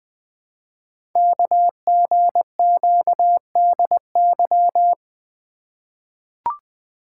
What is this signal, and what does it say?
Morse code, heard as a keyed tone, about 20 wpm, 700 Hz. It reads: KGQDY
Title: Code Groups - 5 Letters - 20wpm